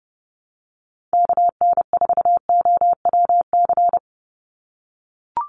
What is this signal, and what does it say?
Morse code, heard as a keyed tone, about 30 wpm, 700 Hz. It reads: KN4OWC
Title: USA Call Signs 30wpm